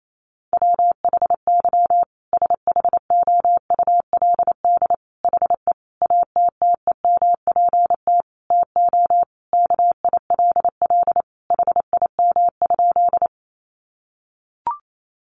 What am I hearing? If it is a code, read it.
Morse code, heard as a keyed tone, about 28 wpm, 700 Hz. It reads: WHY SHOULD HE ATTEMPT TO KILL HIM?
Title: Sentences from Top 1400 Words 28wpm